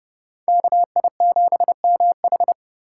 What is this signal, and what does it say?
Morse code, heard as a keyed tone, about 28 wpm, 700 Hz. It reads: KI7MH